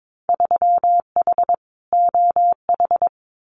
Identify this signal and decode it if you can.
Morse code, heard as a keyed tone, about 22 wpm, 700 Hz. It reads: 3H OH
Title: Winter Field Day Exchanges Encore 22wpm